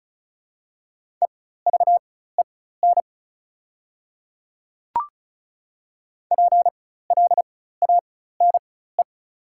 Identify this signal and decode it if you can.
Morse code, heard as a keyed tone, about 34 wpm, spaced as for 16 wpm, 690 Hz. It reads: EVEN PLANE
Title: 4x Character Spacing - Top 500 Words - 35wpm